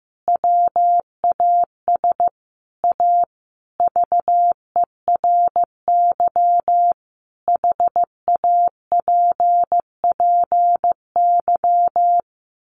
Morse code, heard as a keyed tone, about 15 wpm, 700 Hz. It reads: WAS A VERY HAPPY
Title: Sentences from Top 1400 Words 15wpm